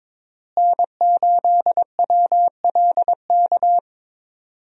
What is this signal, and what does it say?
Morse code, heard as a keyed tone, about 21 wpm, 700 Hz. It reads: N8WLK
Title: USA Call Signs - 5 Characters - Repeated 3x - 22wpm